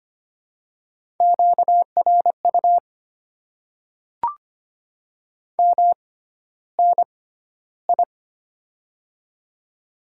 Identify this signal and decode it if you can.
Morse code, heard as a keyed tone, about 25 wpm, spaced as for 13 wpm, 700 Hz. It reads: QRU MNI
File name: Top 90 QSO Elements - 6x Character Spacing - 25wpm